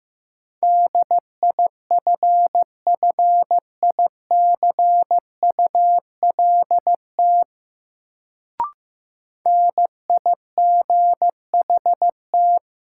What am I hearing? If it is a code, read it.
Morse code, heard as a keyed tone, about 15 wpm, 700 Hz. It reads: DIFFICULT NIGHT
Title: Sets of 4 Words - New Word Drill - 15wpm